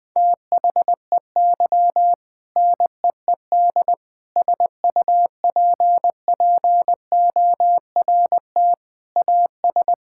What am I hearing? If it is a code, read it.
Morse code, heard as a keyed tone, about 20 wpm, 700 Hz. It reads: THEY NEED SUPPORT AS
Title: Sentences from Top 1000 Words - Review Part 5 - 20wpm